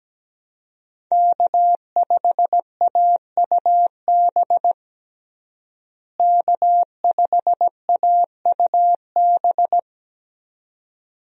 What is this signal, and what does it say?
Morse code, heard as a keyed tone, about 17 wpm, 700 Hz. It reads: K5AUB K5AUB
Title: USA Call Signs - 5 Characters - Repeated 2x - 17wpm